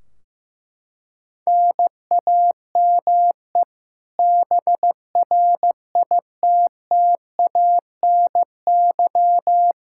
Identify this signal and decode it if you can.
Morse code, heard as a keyed tone, about 15 wpm, 700 Hz. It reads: NAME BRITTANY
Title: Practice QSO - Exchange 1 - Beginning - 15wpm